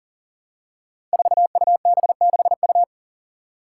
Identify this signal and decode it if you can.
Morse code, heard as a keyed tone, about 43 wpm, 700 Hz. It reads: 4UB6U
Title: Five Challenging Letter-Number Combinations 40wpm